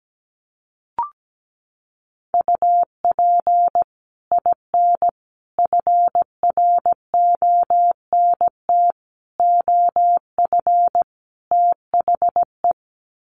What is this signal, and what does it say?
Morse code, heard as a keyed tone, about 17 wpm, 700 Hz. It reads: UP IN FRONT OF THE
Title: Sets of 5 Words Encore2 17wpm